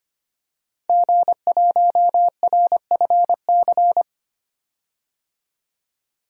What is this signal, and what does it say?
Morse code, heard as a keyed tone, about 25 wpm, 700 Hz. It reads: G1RFC
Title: DXCC-100 Call Signs - 5 Characters - 25wpm